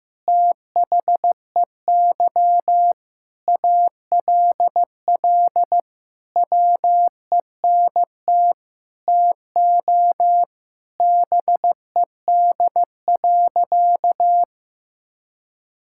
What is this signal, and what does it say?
Morse code, heard as a keyed tone, about 15 wpm, 700 Hz. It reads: THEY ALL WENT TO BED.
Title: Sentences from Top 1000 Words - Review Part 6 - 15wpm